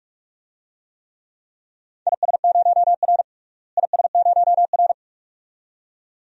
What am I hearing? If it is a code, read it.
Morse code, heard as a keyed tone, about 42 wpm, 700 Hz. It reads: IS0R IS0R
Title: DXCC-100 Call Signs - 4 Characters - Repeated 2x - 45wpm